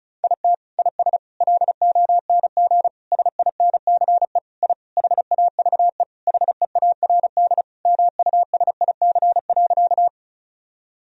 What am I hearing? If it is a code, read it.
Morse code, heard as a keyed tone, about 36 wpm, 700 Hz. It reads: IT IS LONG SINCE I HAVE HEARD MUSIC.
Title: Sentences from Top 1000 Words - Review Part 1 - 35wpm